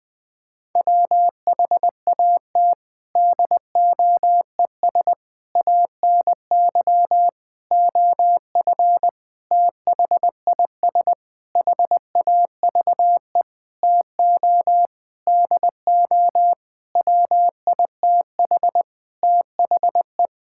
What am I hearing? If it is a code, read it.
Morse code, heard as a keyed tone, about 21 wpm, 700 Hz. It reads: WHAT DOES ANY OF THIS HAVE TO DO WITH THE
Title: Sentences with 9-Word Phrases - Spoken Beginning - 20wpm